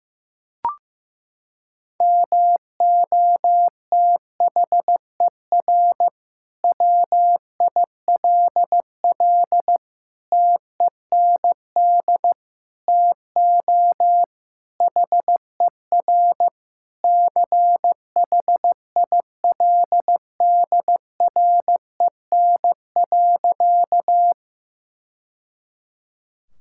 Morse code, heard as a keyed tone, about 15 wpm, 700 Hz. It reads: MOTHER WILL TEND TO HER CHILDREN.
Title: Sentences from Top 1000 Words - Review Part 5 - 15wpm